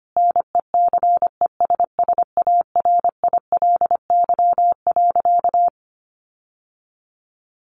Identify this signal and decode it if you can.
Morse code, heard as a keyed tone, about 25 wpm, 700 Hz. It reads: NECESSARILY.
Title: Sentences from Top 1400 Words 25wpm